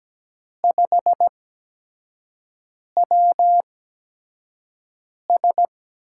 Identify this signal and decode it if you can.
Morse code, heard as a keyed tone, about 17 wpm, spaced as for 5 wpm, 700 Hz. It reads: 5WS
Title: USA Call Signs - 4 Characters - 8x Character Spacing - 17wpm